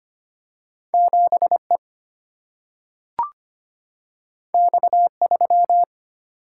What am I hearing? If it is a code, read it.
Morse code, heard as a keyed tone, about 25 wpm, 700 Hz. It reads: 7 E X 3
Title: Single-Letter Single-Number 25wpm